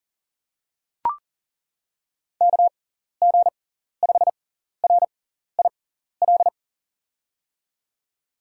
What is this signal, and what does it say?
Morse code, heard as a keyed tone, about 40 wpm, spaced as for 14 wpm, 700 Hz. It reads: KG5RIL
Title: USA Call Signs - 6 Characters - 6x Character Spacing - 40wpm